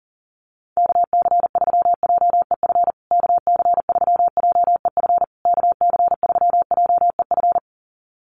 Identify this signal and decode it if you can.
Morse code, heard as a keyed tone, about 40 wpm, 700 Hz. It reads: KC3JEF KC3JEF KC3JEF
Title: USA Call Signs Repeated 3x 40wpm